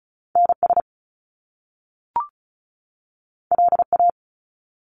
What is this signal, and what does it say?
Morse code, heard as a keyed tone, about 36 wpm, 700 Hz. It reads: NS LA
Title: Top Two Letter Combinations in Words 35wpm